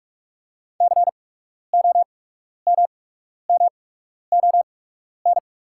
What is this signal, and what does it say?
Morse code, heard as a keyed tone, about 44 wpm, spaced as for 12 wpm, 690 Hz. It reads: COMMON
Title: 8x Character Spacing - Top 500 Words - 45wpm